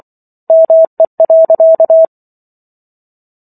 Morse code, heard as a keyed tone, about 24 wpm, 650 Hz. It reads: ME.